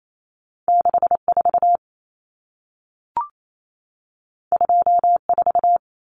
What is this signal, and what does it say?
Morse code, heard as a keyed tone, about 28 wpm, 700 Hz. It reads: 64 24
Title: Numbers - 2-Digit 28wpm